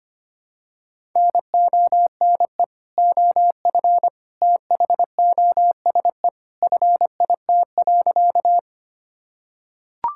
Morse code, heard as a keyed tone, about 25 wpm, 700 Hz. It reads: NONE OF THOSE FIT.
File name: practice / 25wpm / Sentences with 2-Word Phrases - 25wpm